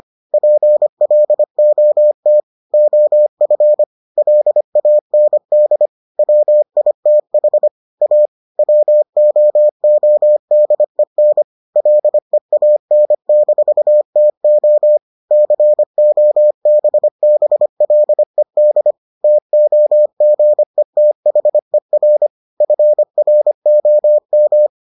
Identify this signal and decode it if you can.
Morse code, heard as a keyed tone, about 25 wpm, 600 Hz. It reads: PLOT OF LAND WITH A WOODEN LEAN-TO COBBLED TOGETHER FROM